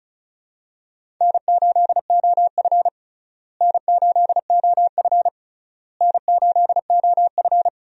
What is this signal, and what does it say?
Morse code, heard as a keyed tone, about 36 wpm, 700 Hz. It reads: N8OF N8OF N8OF
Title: USA Call Signs - 4 Characters - Repeated 3x - 35wpm